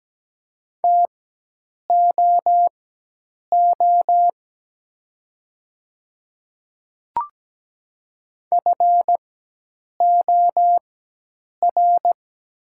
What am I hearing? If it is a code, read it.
Morse code, heard as a keyed tone, about 17 wpm, spaced as for 8 wpm, 700 Hz. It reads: TOO FOR
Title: 4x Character Spacing - Top 200 Words - 17wpm